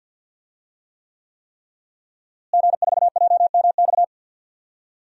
Morse code, heard as a keyed tone, about 46 wpm, 700 Hz. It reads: G4JMX
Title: DXCC-100 Call Signs - 5 Characters - Repeated 3x - 50wpm